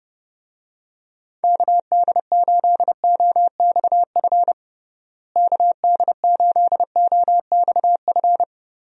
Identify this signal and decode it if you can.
Morse code, heard as a keyed tone, about 30 wpm, 700 Hz. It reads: KD8OXF KD8OXF
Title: USA Call Signs - 6 Characters - Repeated 3x - 30wpm